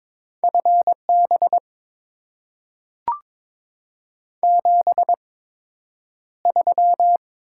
Morse code, heard as a keyed tone, about 22 wpm, spaced as for 9 wpm, 700 Hz. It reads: FB 73
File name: Top 20 QSO Elements - 8x Character Spacing - 22wpm